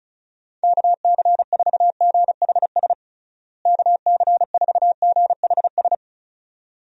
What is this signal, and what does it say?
Morse code, heard as a keyed tone, about 34 wpm, 700 Hz. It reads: KC4GHS KC4GHS
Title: USA Call Signs - 6 Characters - Repeated 2x - 35wpm